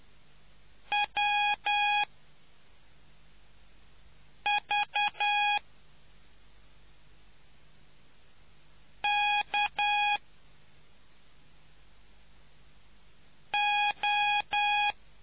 Morse code, heard as a keyed tone, about 10 wpm, spaced as for 3 wpm, 830 Hz. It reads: WV K O